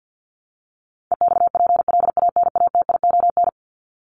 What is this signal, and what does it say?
Morse code, heard as a keyed tone, about 50 wpm, 700 Hz. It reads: EXPLANATION
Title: Top 1751-2000 Words 50wpm